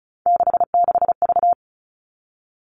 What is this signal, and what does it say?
Morse code, heard as a keyed tone, about 34 wpm, 700 Hz. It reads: 66V
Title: All Cut Numbers - 3-Digit 35wpm